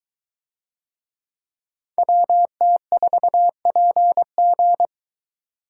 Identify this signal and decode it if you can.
Morse code, heard as a keyed tone, about 21 wpm, 700 Hz. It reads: WT4PG